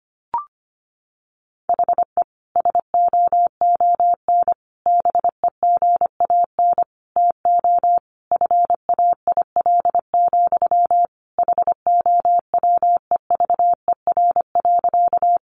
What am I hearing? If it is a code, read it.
Morse code, heard as a keyed tone, about 25 wpm, 700 Hz. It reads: HE SOON BEGAN TO FAIL, HOWEVER.
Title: Sentences from Top 1100 Words 25wpm